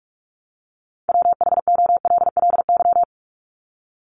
Morse code, heard as a keyed tone, about 44 wpm, 700 Hz. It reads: WHOLLY
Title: LY Suffix - Top 100 Words with Base Words 45wpm